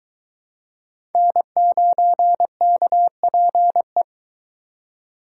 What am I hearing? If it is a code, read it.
Morse code, heard as a keyed tone, about 20 wpm, 700 Hz. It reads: N9KPE